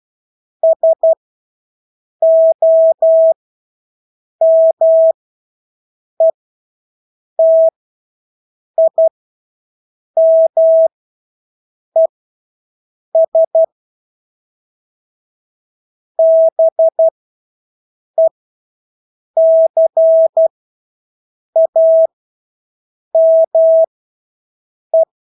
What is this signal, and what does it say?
Morse code, heard as a keyed tone, about 12 wpm, spaced as for 6 wpm, 650 Hz. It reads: SOMETIMES BECAME
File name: Engelske ord 12-6 wpm